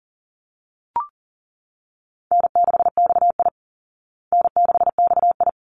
Morse code, heard as a keyed tone, about 40 wpm, 700 Hz. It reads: N6XI N6XI
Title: Easy Call Signs Repeated 2x - 40wpm